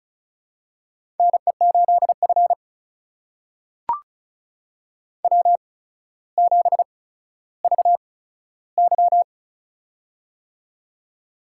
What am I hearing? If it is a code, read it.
Morse code, heard as a keyed tone, about 36 wpm, spaced as for 15 wpm, 700 Hz. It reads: NE8F W7VY
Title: USA Call Signs - 4 Characters - 8x Character Spacing - 35wpm